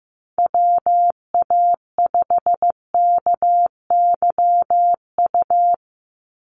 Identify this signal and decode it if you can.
Morse code, heard as a keyed tone, about 15 wpm, 700 Hz. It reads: WA5KYU
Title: USA Call Signs 15wpm